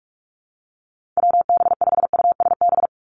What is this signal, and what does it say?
Morse code, heard as a keyed tone, about 46 wpm, 700 Hz. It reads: WB5USB
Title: NA SOTA Chaser Callsigns 45wpm